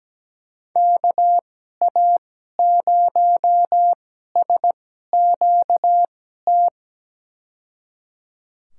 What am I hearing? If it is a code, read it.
Morse code, heard as a keyed tone, about 17 wpm, spaced as for 12 wpm, 700 Hz. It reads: KA0SQT
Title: USA Call Signs - 6 Characters - 2x Character Spacing - 17wpm